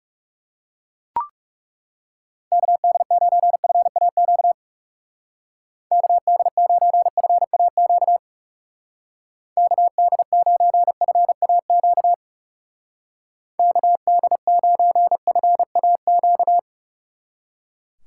Morse code, heard as a keyed tone, about 37 wpm, 700 Hz. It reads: KD9FAQ KD9FAQ KD9FAQ KD9FAQ
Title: Speed Racing - USA Call Signs 30wpm